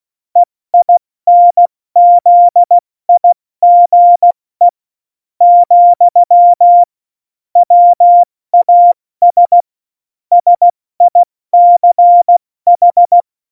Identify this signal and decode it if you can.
Morse code, heard as a keyed tone, about 16 wpm, 700 Hz. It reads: EINZIGE , WAS SICH
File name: Morsekurs / Witze / Witz 02. Januar 2018, 70 F80